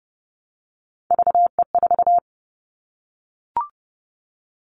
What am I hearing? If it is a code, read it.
Morse code, heard as a keyed tone, about 30 wpm, 700 Hz. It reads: VE4
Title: Callsign Prefixes for Top 100 DXCC - 30wpm